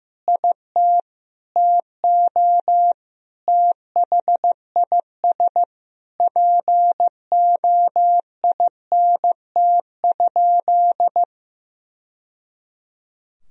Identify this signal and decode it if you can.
Morse code, heard as a keyed tone, about 16 wpm, 700 Hz. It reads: IT TO THIS POINT?
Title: Sentences from Top 1000 Words - Review Part 1 - 15wpm